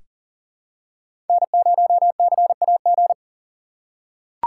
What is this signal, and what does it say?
Morse code, heard as a keyed tone, about 40 wpm, 700 Hz. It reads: N0CAG
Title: USA Call Signs - 5 Characters - 3x Character Spacing - 40wpm